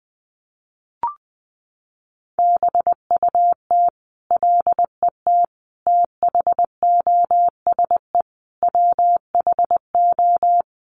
Morse code, heard as a keyed tone, about 20 wpm, 700 Hz. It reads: BUT LET THOSE WHO
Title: Sentences from Top 1500 Words 20wpm